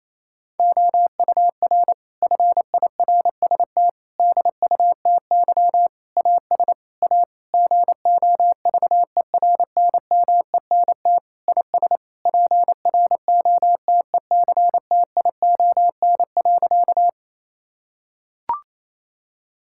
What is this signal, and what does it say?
Morse code, heard as a keyed tone, about 28 wpm, 700 Hz. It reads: OUR FIRST DUTY AS A GOVERNMENT IS PROTECTION.
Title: Sentences from Top 1600 Words 28wpm